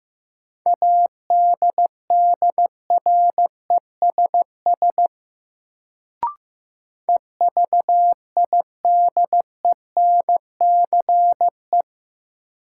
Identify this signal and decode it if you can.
Morse code, heard as a keyed tone, about 15 wpm, 700 Hz. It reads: ADDRESS EVIDENCE
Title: Sets of 5 Words Encore - New Word Drill - 15wpm